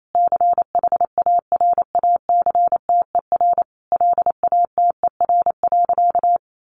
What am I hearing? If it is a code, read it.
Morse code, heard as a keyed tone, about 28 wpm, 700 Hz. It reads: CHARACTER LATER.
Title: Sentences from Top 1200 Words 28wpm